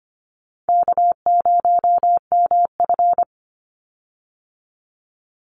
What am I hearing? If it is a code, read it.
Morse code, heard as a keyed tone, about 25 wpm, 700 Hz. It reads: K0MF
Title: USA Call Signs 25wpm